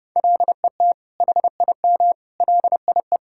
Morse code, heard as a keyed tone, about 30 wpm, 700 Hz. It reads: LET HIM LIE
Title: Sentences from Top 1600 Words 30wpm